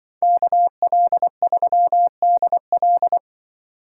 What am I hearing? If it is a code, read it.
Morse code, heard as a keyed tone, about 24 wpm, 700 Hz. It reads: KL3DL
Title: Speed Racing - USA Call Signs 20wpm